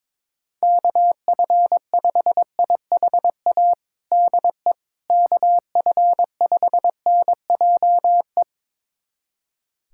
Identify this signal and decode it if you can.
Morse code, heard as a keyed tone, about 22 wpm, 700 Hz. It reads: KF5IHA DE KF5NJE
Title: USA Callsign DE Callsign 22wpm